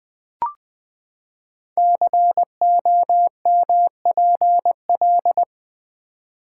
Morse code, heard as a keyed tone, about 20 wpm, 700 Hz. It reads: COMPL
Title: Build A Word - 5 Letters At A Time - 12-Letter Words - 20wpm